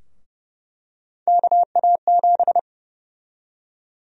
Morse code, heard as a keyed tone, about 30 wpm, 700 Hz. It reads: KA7
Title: USA Call Sign Prefixes - 30wpm